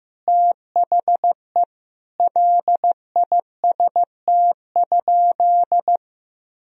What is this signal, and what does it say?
Morse code, heard as a keyed tone, about 16 wpm, 700 Hz. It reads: THE LIST?
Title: Sentences from Top 1000 Words - Review Part 2 - 15wpm